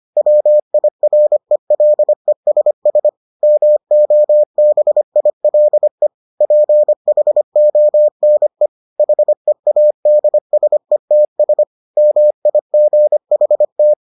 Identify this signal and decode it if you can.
Morse code, heard as a keyed tone, about 25 wpm, 600 Hz. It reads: WIRELESS MOBILE PHONE HEADSETS MIGHT